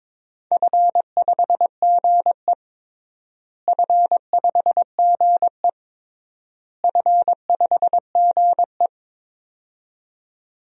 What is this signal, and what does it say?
Morse code, heard as a keyed tone, about 21 wpm, 700 Hz. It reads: F5GE F5GE F5GE
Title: DXCC-100 Call Signs - 4 Characters - Repeated 3x - 22wpm